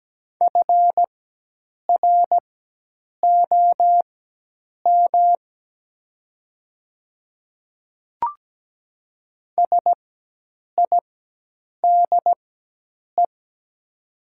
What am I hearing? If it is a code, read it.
Morse code, heard as a keyed tone, about 17 wpm, spaced as for 8 wpm, 700 Hz. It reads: FROM SIDE
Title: 4x Character Spacing - Top 100 Words - 17wpm